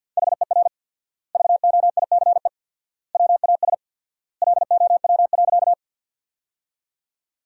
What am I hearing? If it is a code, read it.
Morse code, heard as a keyed tone, about 47 wpm, 700 Hz. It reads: HER VOICE WAS LOW.
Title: 4x Word-Spacing - Sentences from Top 500 Words - 50wpm